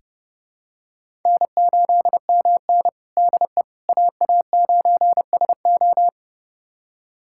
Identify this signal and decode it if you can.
Morse code, heard as a keyed tone, about 30 wpm, 700 Hz. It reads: N8MN DE AA9SO
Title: USA Callsign DE Callsign 30wpm